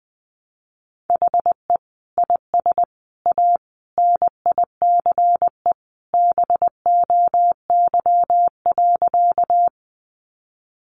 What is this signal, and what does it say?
Morse code, heard as a keyed tone, about 20 wpm, 700 Hz. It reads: HE IS A NICE BOY.
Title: Sentences from Top 1100 Words 20wpm